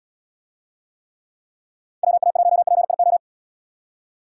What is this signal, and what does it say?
Morse code, heard as a keyed tone, about 77 wpm, 700 Hz. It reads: WA1PIO